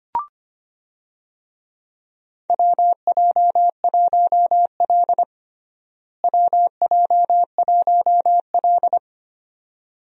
Morse code, heard as a keyed tone, about 25 wpm, 700 Hz. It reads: WJ1L WJ1L
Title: USA Call Signs - 4 Characters - Repeated 2x - 25wpm